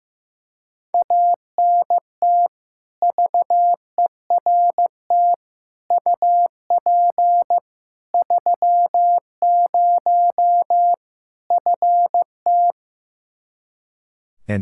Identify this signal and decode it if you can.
Morse code, heard as a keyed tone, about 16 wpm, 700 Hz. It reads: ANT VERT UP 30 FT
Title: K1USN SST QSO Practice 15wpm